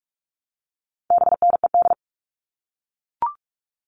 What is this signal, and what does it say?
Morse code, heard as a keyed tone, about 44 wpm, 700 Hz. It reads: BNED
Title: All Cut Numbers - 4-Digit 45wpm